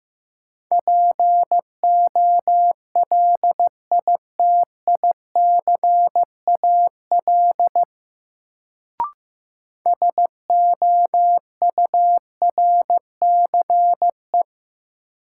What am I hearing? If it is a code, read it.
Morse code, heard as a keyed tone, about 16 wpm, 700 Hz. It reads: POLITICAL SOURCE
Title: Sentences from Top 800 Words - New Word Drill - 15wpm